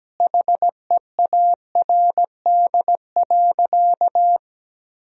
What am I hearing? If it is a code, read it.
Morse code, heard as a keyed tone, about 17 wpm, 700 Hz. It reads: HEARD.
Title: Sentences from Top 1000 Words - Review Part 1 - 17wpm